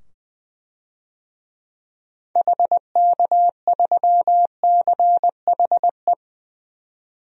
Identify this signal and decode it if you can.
Morse code, heard as a keyed tone, about 20 wpm, 700 Hz. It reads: HK3CHE
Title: DXCC-100 Call Signs - 6 Characters - 20wpm